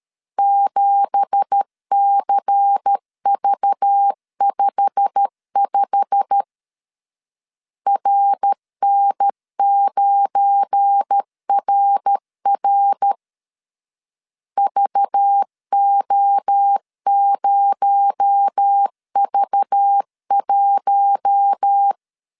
Morse code, heard as a keyed tone, about 13 wpm, 800 Hz. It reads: 7CV55 RN9RR VO0V1
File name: CW / DET / 1 9 1500C S C 12wpm 3811